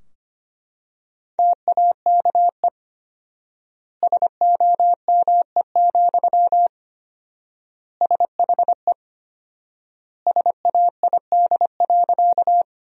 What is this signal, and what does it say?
Morse code, heard as a keyed tone, about 25 wpm, 700 Hz. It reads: TAKE SOME, SHE SAID.
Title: 4x Word-Spacing - Sentences from Top 200 Words - 25wpm